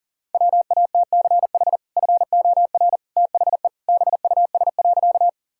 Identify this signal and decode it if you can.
Morse code, heard as a keyed tone, about 40 wpm, 700 Hz. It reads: WATCH FOR THE BUS.
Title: Sentences from Top 1000 Words - Review Part 6 - 40wpm